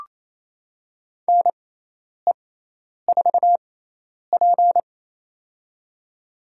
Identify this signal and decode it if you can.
Morse code, heard as a keyed tone, about 28 wpm, spaced as for 10 wpm, 700 Hz. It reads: NE4P
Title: USA Call Signs - 4 Characters - 6x Character Spacing - 28wpm